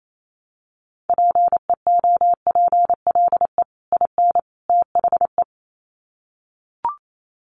Morse code, heard as a keyed tone, about 28 wpm, 700 Hz. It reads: PEOPLE IN THE
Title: Sets of 3 Words 28wpm